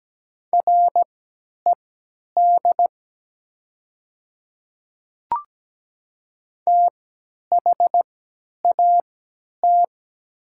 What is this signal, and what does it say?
Morse code, heard as a keyed tone, about 17 wpm, spaced as for 10 wpm, 700 Hz. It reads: RED THAT
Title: 3x Character Spacing - Top 500 Words - 17wpm